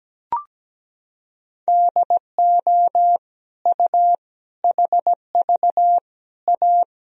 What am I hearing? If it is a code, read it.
Morse code, heard as a keyed tone, about 17 wpm, 700 Hz. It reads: DO U HV A
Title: Sentences from Top 300 Words and Abbreviations 17wpm